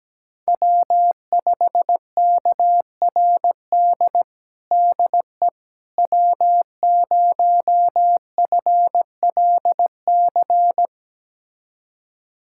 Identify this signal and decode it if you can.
Morse code, heard as a keyed tone, about 17 wpm, 700 Hz. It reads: W5KRD DE W0FLC
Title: USA Callsign DE Callsign - 5 Characters - 17wpm